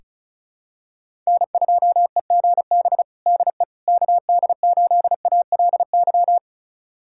Morse code, heard as a keyed tone, about 34 wpm, 690 Hz. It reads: N2EGB DE KD8ALY
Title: USA Callsign DE Callsign 35wpm